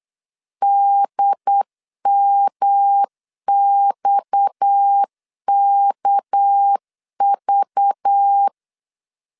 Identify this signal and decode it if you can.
Morse code, heard as a keyed tone, about 8 wpm, 800 Hz. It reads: DMXKV
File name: LEZIONI / 8 YJ45 / 8 YJ 8wpm 3313